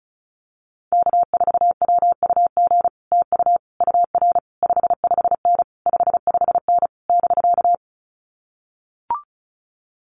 Morse code, heard as a keyed tone, about 36 wpm, 700 Hz. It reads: K4WUG TU UR 55N 55N <BK>
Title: Common QSO Phrases Encore 35wpm